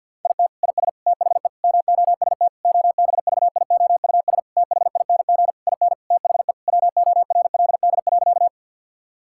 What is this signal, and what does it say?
Morse code, heard as a keyed tone, about 50 wpm, 700 Hz. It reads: IT IS THE MOST OBVIOUS THING IN THE WORLD.
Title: Sentences from Top 1600 Words 50wpm